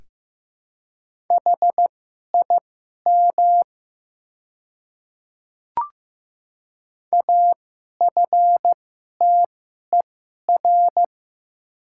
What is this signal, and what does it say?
Morse code, heard as a keyed tone, about 15 wpm, spaced as for 11 wpm, 700 Hz. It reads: HIM AFTER
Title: 2x Character Spacing - Top 200 Words - 15wpm